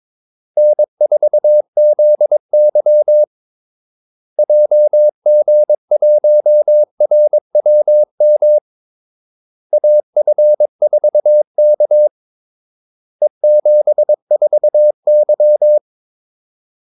Morse code, heard as a keyed tone, about 22 wpm, 600 Hz. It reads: N4ZY JG1RWM AF4K E74Y
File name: Callsigns-22wpm